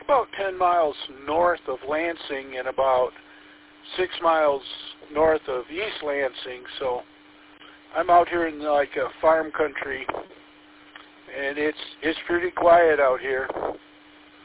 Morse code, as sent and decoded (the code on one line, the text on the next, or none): none